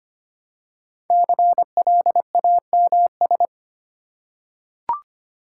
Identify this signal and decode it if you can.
Morse code, heard as a keyed tone, about 25 wpm, 700 Hz. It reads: CLAMS
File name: practice / 25wpm / Difficult Words with Hints - 25wpm